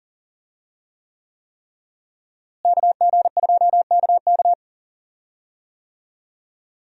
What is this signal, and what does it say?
Morse code, heard as a keyed tone, about 40 wpm, 700 Hz. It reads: KG2KK
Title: USA Call Signs - 5 Characters - Repeated 3x - 40wpm